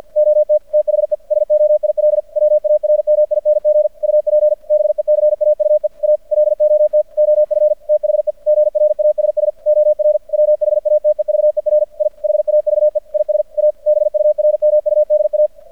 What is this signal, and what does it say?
Morse code, heard as a keyed tone, about 52 wpm, 600 Hz. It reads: OT THE SOIL WARMING UP BECAUE A LOT OF THE GRASS OR WHATEVER I HAVE IS A BURMUDA
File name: archived / CRSnet